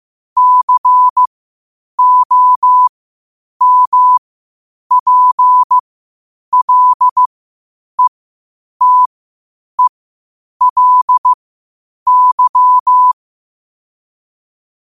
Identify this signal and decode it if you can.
Morse code, heard as a keyed tone, about 15 wpm, spaced as for 8 wpm, 1000 Hz. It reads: COMPLETELY
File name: qotdfiles / 8wpm